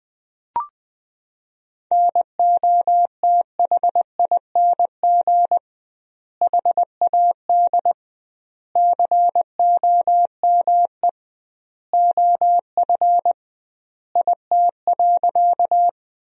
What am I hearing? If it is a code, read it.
Morse code, heard as a keyed tone, about 21 wpm, 700 Hz. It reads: NOTHING HAD COME OF IT.
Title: 2x Word-Spacing - Sentences from Top 500 Words - 20wpm